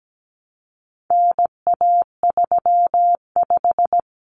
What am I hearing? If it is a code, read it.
Morse code, heard as a keyed tone, about 17 wpm, 700 Hz. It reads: NA35
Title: Common Cut Numbers - 4-Digit 17wpm